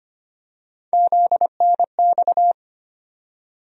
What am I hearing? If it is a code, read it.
Morse code, heard as a keyed tone, about 25 wpm, 700 Hz. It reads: ZNX
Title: USA Call Sign Postfixes - 3 Letters 25wpm